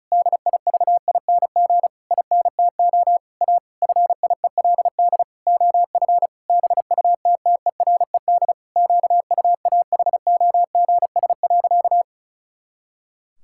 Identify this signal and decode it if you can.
Morse code, heard as a keyed tone, about 36 wpm, 700 Hz. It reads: DIVING INTO A FIELD OF BUTTERED QUAHOGS.
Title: Pangram Challenge v2 - 35wpm